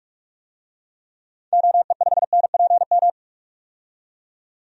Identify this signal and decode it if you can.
Morse code, heard as a keyed tone, about 43 wpm, 700 Hz. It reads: OE5NPM
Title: DXCC-100 Call Signs - 6 Characters - 45wpm